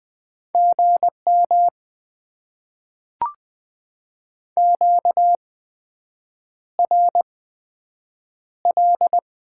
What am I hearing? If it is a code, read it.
Morse code, heard as a keyed tone, about 20 wpm, 700 Hz. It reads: G M Q R L
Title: Top 20 QSO Elements - 8x Character Spacing - 20wpm